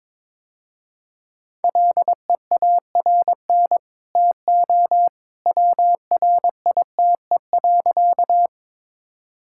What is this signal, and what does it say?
Morse code, heard as a keyed tone, about 22 wpm, 700 Hz. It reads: LEARN TO WRITE.
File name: practice / 22wpm / Easy Sentences with Hints - 22wpm